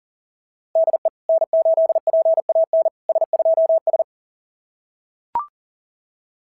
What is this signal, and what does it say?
Morse code, heard as a keyed tone, about 40 wpm, 690 Hz. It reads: DE N8PAN S2S
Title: SOTA QSO Practice 40wpm